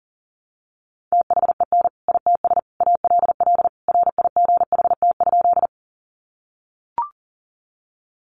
Morse code, heard as a keyed tone, about 40 wpm, 700 Hz. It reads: THEN ITS ALL RIGHT?
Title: Sentences from Top 600 Words 40wpm